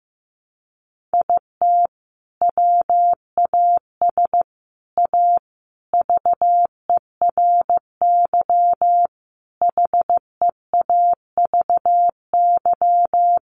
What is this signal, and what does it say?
Morse code, heard as a keyed tone, about 15 wpm, 700 Hz. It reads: IT WAS A VERY HEAVY
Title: Sentences from Top 1100 Words 15wpm